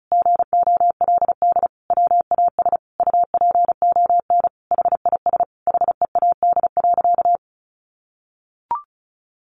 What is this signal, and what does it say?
Morse code, heard as a keyed tone, about 34 wpm, 700 Hz. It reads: GOLD WAS UPON HIS HEAD.
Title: Sentences from Top 1100 Words 35wpm